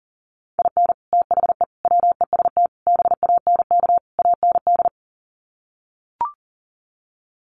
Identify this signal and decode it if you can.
Morse code, heard as a keyed tone, about 40 wpm, 700 Hz. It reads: IN THE WEST BANK AND
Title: Sets of 5 Words Encore2 40wpm